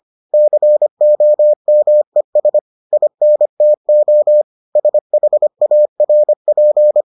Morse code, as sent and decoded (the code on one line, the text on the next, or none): -.-. --- -- . ... / .. -. - --- / ... .... .- .-. .--.
COMES INTO SHARP